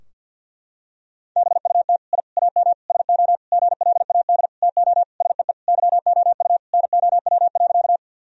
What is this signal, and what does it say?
Morse code, heard as a keyed tone, about 50 wpm, 700 Hz. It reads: BUT I AM SO GLAD TO SEE YOU NOW.
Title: Sentences from Top 1000 Words - Review Part 2 - 50wpm